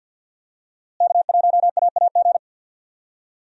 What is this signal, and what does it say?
Morse code, heard as a keyed tone, about 47 wpm, 700 Hz. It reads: K1AAG
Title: USA Call Signs - 5 Characters - Repeated 3x - 50wpm